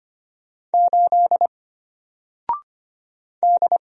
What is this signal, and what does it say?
Morse code, heard as a keyed tone, about 25 wpm, 700 Hz. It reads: 8 D
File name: Single-Letter-Number-25wpm